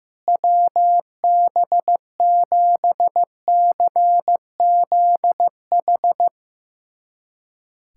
Morse code, heard as a keyed tone, about 15 wpm, 700 Hz. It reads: WB7CZH
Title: USA Callsign DE Callsign 15wpm